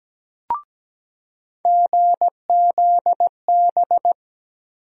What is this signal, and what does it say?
Morse code, heard as a keyed tone, about 17 wpm, 700 Hz. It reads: GZB
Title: USA Call Sign Postfixes - 3 Letters 17wpm